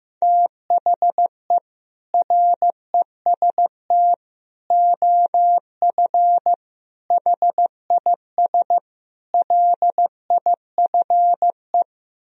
Morse code, heard as a keyed tone, about 15 wpm, 700 Hz. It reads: THE REST OF HIS LIFE
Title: Speed-Racing - Sets of 5 Words 15wpm